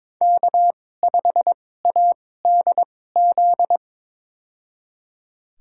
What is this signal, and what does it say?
Morse code, heard as a keyed tone, about 22 wpm, spaced as for 16 wpm, 700 Hz. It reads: K5ADZ
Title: USA Call Signs - 2x Character Spacing - 22wpm